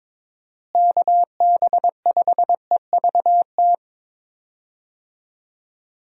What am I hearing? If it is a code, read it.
Morse code, heard as a keyed tone, about 23 wpm, 700 Hz. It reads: KB5EVT
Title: USA Call Signs - 6 Characters - Repeated 3x - 22wpm